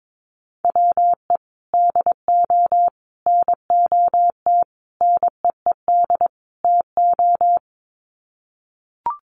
Morse code, heard as a keyed tone, about 22 wpm, 700 Hz. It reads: WE DO NOT NEED TO
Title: Sets of 5 Words Encore 22wpm